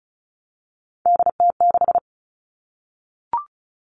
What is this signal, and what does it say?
Morse code, heard as a keyed tone, about 34 wpm, 700 Hz. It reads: DT6
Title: All Cut Numbers - 3-Digit 35wpm